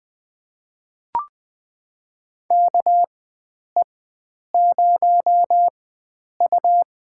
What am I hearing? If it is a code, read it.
Morse code, heard as a keyed tone, about 20 wpm, spaced as for 9 wpm, 700 Hz. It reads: KE0U
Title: USA Call Signs - 4x Character Spacing - 20wpm